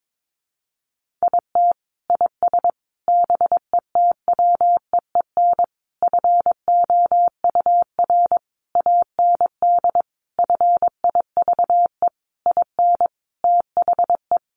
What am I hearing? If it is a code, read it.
Morse code, heard as a keyed tone, about 22 wpm, 700 Hz. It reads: IT IS BETWEEN FOUR AND FIVE IN THE
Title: Sentences from Top 1500 Words 22wpm